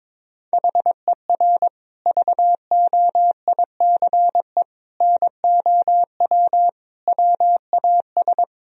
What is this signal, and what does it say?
Morse code, heard as a keyed tone, about 22 wpm, 700 Hz. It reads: HER VOICE NOW WAS
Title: Sentences from Top 1000 Words - Review Part 3 - 22wpm